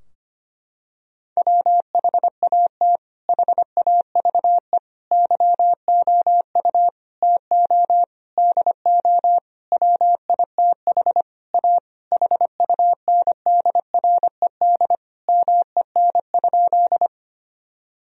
Morse code, heard as a keyed tone, about 25 wpm, 700 Hz. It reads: WHAT HAVE YOU TO DO WITH A HUNDRED MEN?
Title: Sentences from Top 1000 Words - Review Part 2 - 25wpm